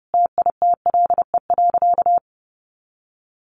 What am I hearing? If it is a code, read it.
Morse code, heard as a keyed tone, about 30 wpm, 700 Hz. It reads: TITLE.
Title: Sentences from Top 1200 Words 30wpm